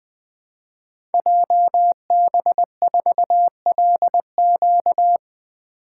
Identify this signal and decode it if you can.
Morse code, heard as a keyed tone, about 20 wpm, 700 Hz. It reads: JB4LQ
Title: Code Groups - 5 Letter:Numbers - 20wpm